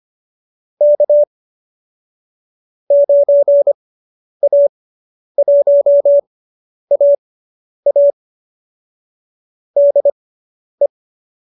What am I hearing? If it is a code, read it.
Morse code, heard as a keyed tone, about 25 wpm, spaced as for 10 wpm, 570 Hz. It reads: K 9A1AA DE